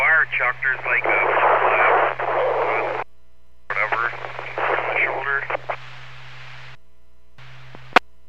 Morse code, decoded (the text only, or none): none